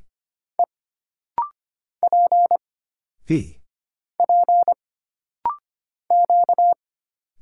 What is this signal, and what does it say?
Morse code, heard as a keyed tone, about 25 wpm, 700 Hz. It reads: E P P Q
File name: Single Character (Letter:Number:Punctuation:Prosign) - Rapid-Fire - 25wpm